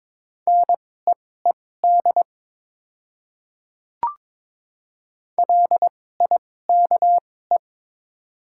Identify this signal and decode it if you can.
Morse code, heard as a keyed tone, about 22 wpm, spaced as for 16 wpm, 700 Hz. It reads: NEED LIKE
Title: 2x Character Spacing - Top 200 Words - 22wpm